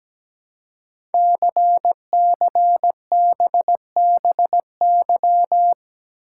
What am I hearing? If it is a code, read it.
Morse code, heard as a keyed tone, about 17 wpm, 700 Hz. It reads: CCBBY
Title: Code Groups - 5 Letters - 17wpm